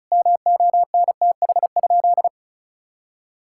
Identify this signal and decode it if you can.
Morse code, heard as a keyed tone, about 36 wpm, 700 Hz. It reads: MONTH?
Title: Sentences from Top 1000 Words - Review Part 5 - 35wpm